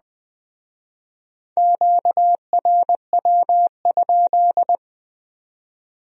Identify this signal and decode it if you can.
Morse code, heard as a keyed tone, about 20 wpm, 700 Hz. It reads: QRW?